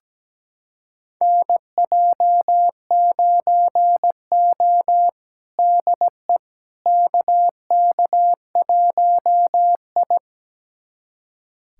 Morse code, heard as a keyed tone, about 17 wpm, 700 Hz. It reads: NJ9O DE KK1I
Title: USA Callsign DE Callsign - 4 Characters - 17wpm